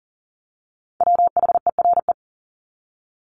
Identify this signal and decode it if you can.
Morse code, heard as a keyed tone, about 38 wpm, 700 Hz. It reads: WHERE
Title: Top 300 Words 40wpm